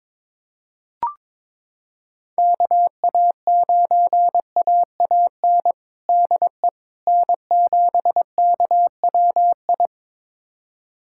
Morse code, heard as a keyed tone, about 22 wpm, 700 Hz. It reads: KA9AAN DE N7KWI
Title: USA Callsign DE Callsign 22wpm